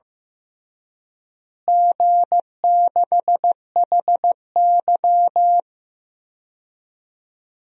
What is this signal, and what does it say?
Morse code, heard as a keyed tone, about 15 wpm, 700 Hz. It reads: G6HY
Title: DXCC-100 Call Signs - 4 Characters - Repeated 2x - 15wpm